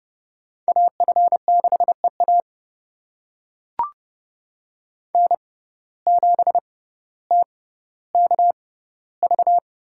Morse code, heard as a keyed tone, about 31 wpm, spaced as for 16 wpm, 700 Hz. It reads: AF6EA N7TKV
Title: USA Call Signs - 5 Characters - 6x Character Spacing - 30wpm